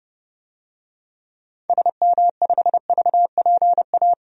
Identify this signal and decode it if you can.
Morse code, heard as a keyed tone, about 32 wpm, 700 Hz. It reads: SM5VPA
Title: DXCC-100 Call Signs - 6 Characters - 30wpm